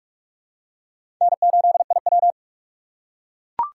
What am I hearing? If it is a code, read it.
Morse code, heard as a keyed tone, about 44 wpm, 700 Hz. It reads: N8IW
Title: USA Call Signs - 4 Characters - 6x Character Spacing - 45wpm